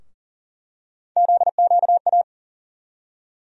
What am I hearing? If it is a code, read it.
Morse code, heard as a keyed tone, about 40 wpm, 700 Hz. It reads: ZQA
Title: USA Call Sign Postfixes - 3 Letters 40wpm